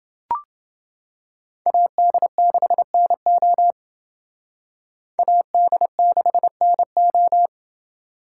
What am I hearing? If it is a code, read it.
Morse code, heard as a keyed tone, about 28 wpm, 700 Hz. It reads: AD6NO AD6NO